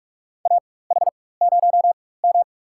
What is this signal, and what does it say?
Morse code, heard as a keyed tone, about 46 wpm, spaced as for 21 wpm, 700 Hz. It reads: AH0M
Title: USA Call Signs - 4 Characters - 4x Character Spacing - 45wpm